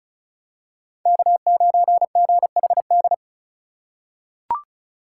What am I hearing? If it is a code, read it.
Morse code, heard as a keyed tone, about 36 wpm, 700 Hz. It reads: K9GHD
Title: USA Call Signs - 3x Character Spacing - 35wpm